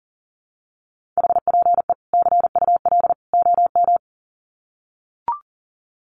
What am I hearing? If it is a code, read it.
Morse code, heard as a keyed tone, about 40 wpm, 700 Hz. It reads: HPE CUL OM
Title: Common QSO Phrases 40wpm